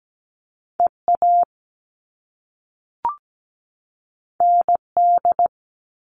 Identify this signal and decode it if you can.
Morse code, heard as a keyed tone, about 17 wpm, 700 Hz. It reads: EA ND
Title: All Cut Numbers - 2-Digit 17wpm